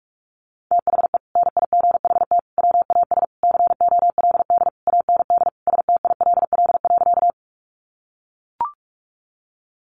Morse code, heard as a keyed tone, about 44 wpm, 700 Hz. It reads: THE NIGHT WAS COLD AND STILL.
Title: Sentences from Top 400 Words 45wpm